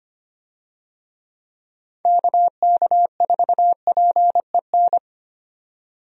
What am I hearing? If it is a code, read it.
Morse code, heard as a keyed tone, about 25 wpm, 700 Hz. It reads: KK4PEN
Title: USA Call Signs - 6 Characters - 25wpm